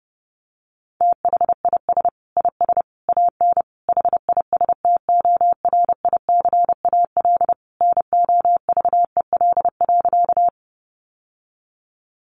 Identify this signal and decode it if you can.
Morse code, heard as a keyed tone, about 30 wpm, 700 Hz. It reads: THIS IS AN HISTORICAL NOVEL.
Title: Sentences from Top 1300 Words 30wpm